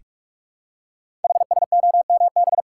Morse code, heard as a keyed tone, about 44 wpm, 700 Hz. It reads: HSOMB
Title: Code Groups - 5 Letters - 45wpm